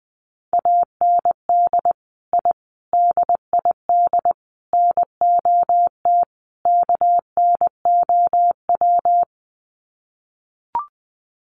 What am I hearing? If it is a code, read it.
Morse code, heard as a keyed tone, about 20 wpm, 700 Hz. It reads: AND I DID NOT KNOW
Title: Sets of 5 Words Encore 20wpm